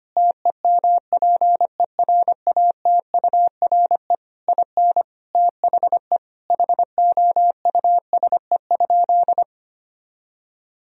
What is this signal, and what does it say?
Morse code, heard as a keyed tone, about 25 wpm, 700 Hz. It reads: TEMPERATURE IN THE HOUSE?
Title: Sentences from Top 1000 Words - Review Part 4 - 25wpm